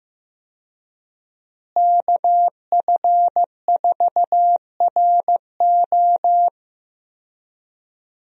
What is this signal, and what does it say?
Morse code, heard as a keyed tone, about 16 wpm, 700 Hz. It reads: KF4RO
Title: USA Call Signs - 5 Characters - Repeated 2x - 15wpm